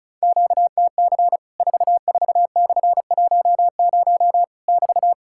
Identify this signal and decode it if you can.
Morse code, heard as a keyed tone, about 34 wpm, 700 Hz. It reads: QTC 44/10 <BT>
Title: Worked All Europe DX Contest - QTC Exchange - 35wpm